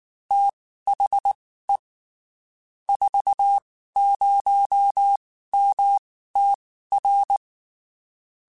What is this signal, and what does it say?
Morse code, heard as a keyed tone, about 19 wpm, spaced as for 14 wpm, 800 Hz. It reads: THE 40MTR
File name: (spaced) / QSOs Spaced-mp3s / WA7QOX 2 @18 sp